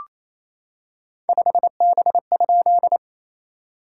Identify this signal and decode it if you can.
Morse code, heard as a keyed tone, about 29 wpm, 690 Hz. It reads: 5B?
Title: SOTA QSO Practice 28wpm